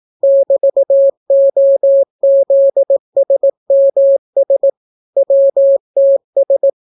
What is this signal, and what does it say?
Morse code, heard as a keyed tone, about 18 wpm, 550 Hz. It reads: <BT>OZSMS WTS